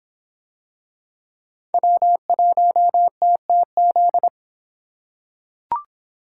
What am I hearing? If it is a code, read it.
Morse code, heard as a keyed tone, about 26 wpm, 700 Hz. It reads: W1TTZ